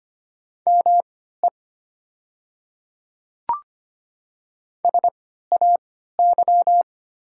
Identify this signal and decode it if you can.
Morse code, heard as a keyed tone, about 25 wpm, spaced as for 14 wpm, 700 Hz. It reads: ME SAY
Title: 3x Character Spacing - Top 300 Words - 25wpm